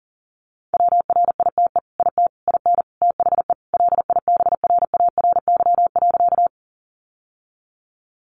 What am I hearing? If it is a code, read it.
Morse code, heard as a keyed tone, about 40 wpm, 700 Hz. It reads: WRITE IT IN THE LIBRARY.
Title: Sentences from Top 600 Words 40wpm